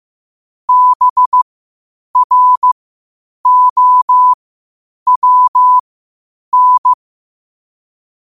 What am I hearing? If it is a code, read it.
Morse code, heard as a keyed tone, about 15 wpm, spaced as for 8 wpm, 1000 Hz. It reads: BROWN
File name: qotdfiles / 8wpm